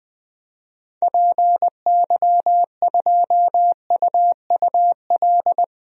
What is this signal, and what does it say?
Morse code, heard as a keyed tone, about 21 wpm, 700 Hz. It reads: PY2UUL
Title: DXCC-100 Call Signs - 6 Characters - 20wpm